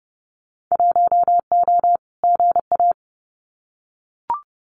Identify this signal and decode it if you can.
Morse code, heard as a keyed tone, about 30 wpm, 700 Hz. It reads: 1O GA
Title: Winter Field Day Exchanges 30wpm